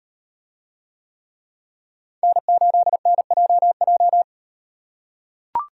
Morse code, heard as a keyed tone, about 38 wpm, 700 Hz. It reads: N8NJJ